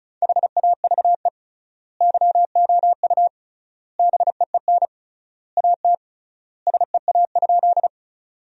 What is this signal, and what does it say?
Morse code, heard as a keyed tone, about 36 wpm, 700 Hz. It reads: HAVE YOU BEEN AT SEA?
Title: 3x Word-Spacing - Sentences from Top 300 Words - 35wpm